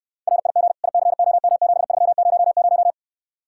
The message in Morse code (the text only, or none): RIG ICOM7300